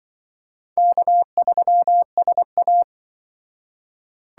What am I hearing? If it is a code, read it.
Morse code, heard as a keyed tone, about 25 wpm, 700 Hz. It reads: K3SA
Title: Speed Racing - USA 1x2 Call Signs 20wpm